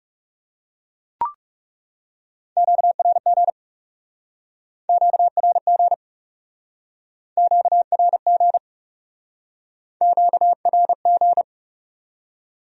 Morse code, heard as a keyed tone, about 38 wpm, 700 Hz. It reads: QRG QRG QRG QRG